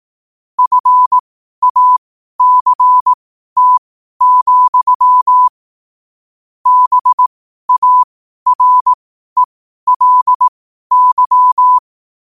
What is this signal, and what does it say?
Morse code, heard as a keyed tone, about 18 wpm, spaced as for 12 wpm, 1000 Hz. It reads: FACT, BARELY
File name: qotdfiles / 12wpm